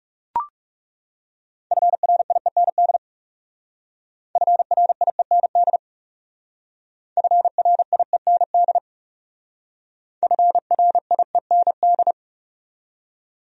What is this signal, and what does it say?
Morse code, heard as a keyed tone, about 38 wpm, 700 Hz. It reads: FRIEND FRIEND FRIEND FRIEND
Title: Speed-Racing Top 300 Words 30wpm